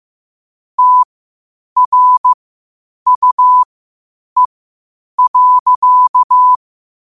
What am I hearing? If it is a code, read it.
Morse code, heard as a keyed tone, about 15 wpm, spaced as for 8 wpm, 1000 Hz. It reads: TRUE.
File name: qotdfiles / 8wpm